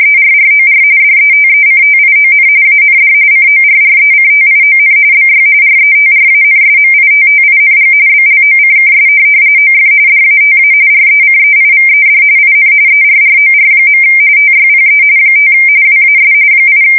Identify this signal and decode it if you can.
RTTY, about 45.45 baud. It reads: ERROR DETECTION AND CORRECTION BY RETRANSMITTING BLOCKS OF DATA RECEIVED IN ERROR. TYPE 5 WAS 8
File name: fsk170 / teletype-corp